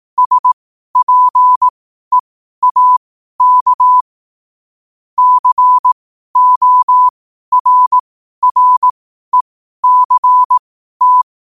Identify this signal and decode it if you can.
Morse code, heard as a keyed tone, about 18 wpm, spaced as for 12 wpm, 1000 Hz. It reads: SPEAK CORRECT
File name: qotdfiles / 12wpm